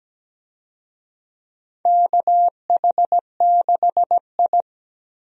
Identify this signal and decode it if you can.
Morse code, heard as a keyed tone, about 17 wpm, 700 Hz. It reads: KH6I
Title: DXCC-100 Call Signs - 4 Characters - 17wpm